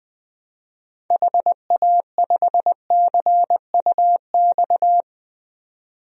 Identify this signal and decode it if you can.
Morse code, heard as a keyed tone, about 20 wpm, 700 Hz. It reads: HA5CUX
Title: DXCC-100 Call Signs - 6 Characters - Repeated 2x - 20wpm